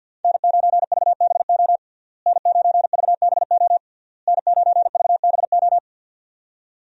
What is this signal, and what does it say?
Morse code, heard as a keyed tone, about 52 wpm, 700 Hz. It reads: N9VBO N9VBO N9VBO
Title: USA Call Signs - 5 Characters - Repeated 3x - 50wpm